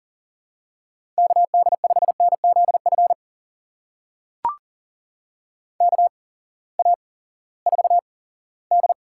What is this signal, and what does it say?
Morse code, heard as a keyed tone, about 38 wpm, spaced as for 20 wpm, 700 Hz. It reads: KD5NZF KA4D
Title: USA Call Signs - 8x Character Spacing - 40wpm